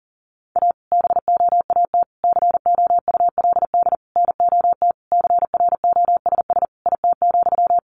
Morse code, heard as a keyed tone, about 40 wpm, 700 Hz. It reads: A BOAT COULD NOT CROSS IT,
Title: Sentences from Top 1400 Words 40wpm